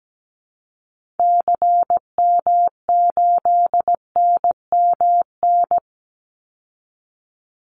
Morse code, heard as a keyed tone, about 17 wpm, 700 Hz. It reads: CM8NMN
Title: Call Signs Repeated 2x - 17wpm